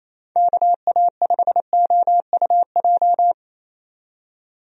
Morse code, heard as a keyed tone, about 28 wpm, 700 Hz. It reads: KA5OUJ
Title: USA Call Signs - 6 Characters - 28wpm